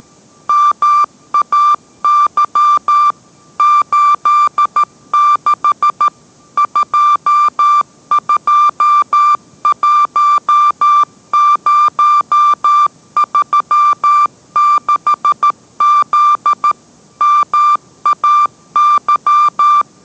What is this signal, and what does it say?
Morse code, heard as a keyed tone, about 13 wpm, 1180 Hz. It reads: MAY 86 221036Z MAY